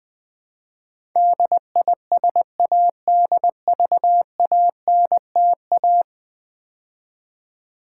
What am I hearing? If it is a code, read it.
Morse code, heard as a keyed tone, about 20 wpm, 700 Hz. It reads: DISADVANTA
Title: Build A Word - 5 Letters At A Time - 12-Letter Words - 20wpm